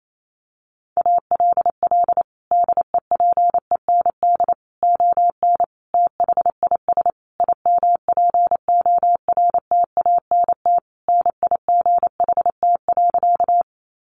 Morse code, heard as a keyed tone, about 28 wpm, 700 Hz. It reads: ALL DEPEND ON THIS IMPORTANT NIGHT.
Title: Sentences from Top 1300 Words 28wpm